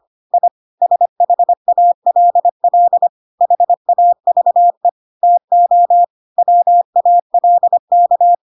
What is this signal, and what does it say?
Morse code, heard as a keyed tone, about 25 wpm, 700 Hz. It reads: I SHALL HAVE TO WALK
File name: Pinocchio / Pinocchio ch7 25 25 ik1hge